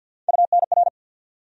Morse code, heard as a keyed tone, about 47 wpm, 700 Hz. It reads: UNR
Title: USA Call Sign Postfixes - 3 Letters 50wpm